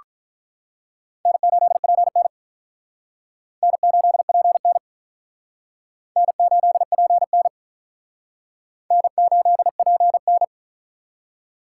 Morse code, heard as a keyed tone, about 44 wpm, 700 Hz. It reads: N8PN N8PN N8PN N8PN